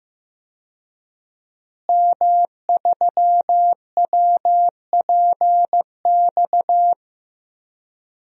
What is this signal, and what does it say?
Morse code, heard as a keyed tone, about 15 wpm, 700 Hz. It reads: M3WPX
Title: DXCC-100 Call Signs - 5 Characters - Repeated 3x - 15wpm